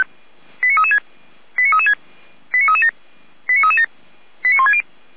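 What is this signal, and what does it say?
Five-tone selective call (ZVEI), heard as quick runs of five tones, tones of about 70 ms. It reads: 89336 89336 89337 89337 89160
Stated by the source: ZVEI five-tone selective call received via SDR.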